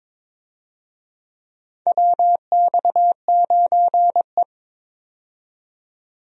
Morse code, heard as a keyed tone, about 22 wpm, 700 Hz. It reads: WX9E
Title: USA Call Signs - 4 Characters - 22wpm